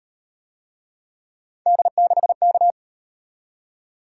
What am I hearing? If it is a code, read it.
Morse code, heard as a keyed tone, about 35 wpm, 700 Hz. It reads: D6K